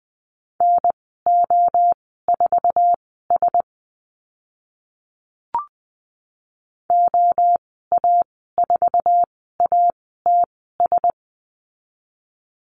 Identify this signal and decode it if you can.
Morse code, heard as a keyed tone, about 20 wpm, spaced as for 14 wpm, 700 Hz. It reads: NO4S OA4ATS
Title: 062 Lesson 11g - Letter h - Call Signs 20wpm